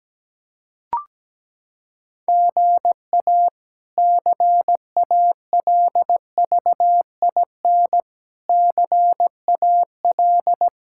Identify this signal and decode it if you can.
Morse code, heard as a keyed tone, about 17 wpm, 700 Hz. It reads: GA CALVIN CAL
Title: K1USN SST QSO Practice - 17wpm